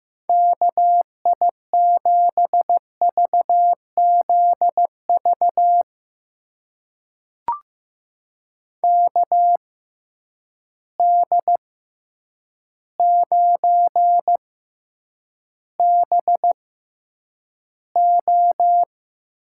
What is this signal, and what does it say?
Morse code, heard as a keyed tone, about 15 wpm, spaced as for 8 wpm, 700 Hz. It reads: KI7VZV KD9BO
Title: USA Call Signs - 6 Characters - 6x Character Spacing - 15wpm